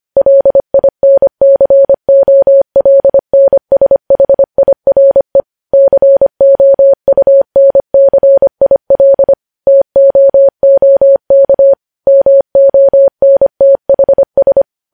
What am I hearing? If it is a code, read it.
Morse code, heard as a keyed tone, about 25 wpm, 550 Hz. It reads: LINCOLNSHIRE COUNCIL TOOK MONTHS
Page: news headlines in morse code at 25 wpm